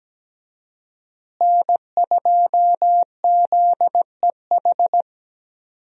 Morse code, heard as a keyed tone, about 17 wpm, 700 Hz. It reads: N2ZEH
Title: USA Call Signs - 5 Characters - Repeated 2x - 17wpm